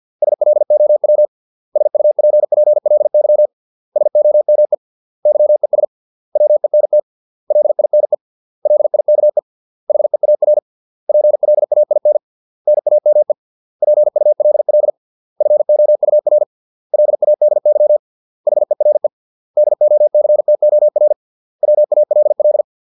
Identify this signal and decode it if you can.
Morse code, heard as a keyed tone, about 50 wpm, 600 Hz. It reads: SLOW SUPPLY SOME YES WENT LINE LIKE HEAR PLAIN NAME PULL FOUR LADY HERE DOCTOR WALL